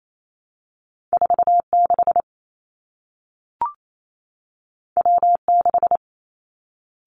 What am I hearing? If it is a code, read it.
Morse code, heard as a keyed tone, about 28 wpm, 700 Hz. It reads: 4 6 W 6
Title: Two Challenging Letter-Number Combinations 28wpm